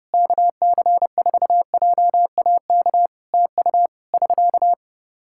Morse code, heard as a keyed tone, about 31 wpm, 700 Hz. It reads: KC4JAK TU <SK>
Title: Practice QSO - Exchange 4 - Ending - 30wpm